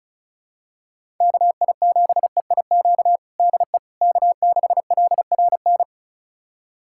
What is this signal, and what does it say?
Morse code, heard as a keyed tone, about 34 wpm, 700 Hz. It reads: KI7EIQ DE K6LRN
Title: USA Callsign DE Callsign 35wpm